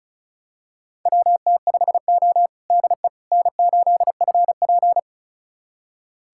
Morse code, heard as a keyed tone, about 36 wpm, 700 Hz. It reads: WT5O DE N8FP
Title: USA Callsign DE Callsign - 4 Characters - 35wpm